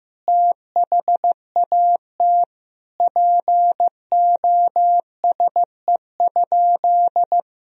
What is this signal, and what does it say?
Morse code, heard as a keyed tone, about 15 wpm, 700 Hz. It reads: THAT POSE?
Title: Sentences from Top 1000 Words - Review Part 2 - 15wpm